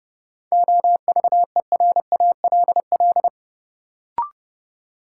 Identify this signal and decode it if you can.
Morse code, heard as a keyed tone, about 30 wpm, 700 Hz. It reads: OVERALL
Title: Sentences from Top 1000 Words - Review Part 7 - New Word Drill - 30wpm